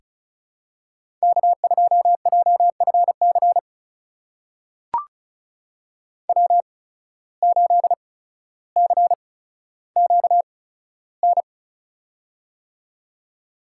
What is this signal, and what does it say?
Morse code, heard as a keyed tone, about 34 wpm, spaced as for 15 wpm, 700 Hz. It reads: K2JFC W8CQN
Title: USA Call Signs - 5 Characters - 8x Character Spacing - 35wpm